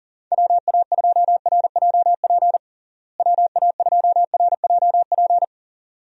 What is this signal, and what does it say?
Morse code, heard as a keyed tone, about 40 wpm, 700 Hz. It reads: WA2RJP WA2RJP
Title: USA Call Signs - 6 Characters - Repeated 2x - 40wpm